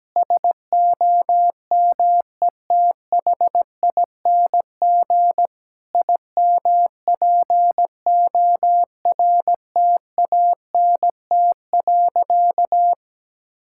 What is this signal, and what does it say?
Morse code, heard as a keyed tone, about 17 wpm, 700 Hz. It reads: SOMETHING IMPORTANT.
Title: Sentences from Top 1000 Words - Review Part 6 - 17wpm